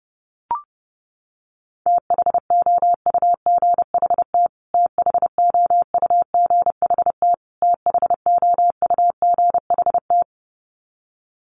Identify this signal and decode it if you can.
Morse code, heard as a keyed tone, about 28 wpm, 700 Hz. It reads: THOUGHT THOUGHT THOUGHT
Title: Top 300 Words Repeated 30wpm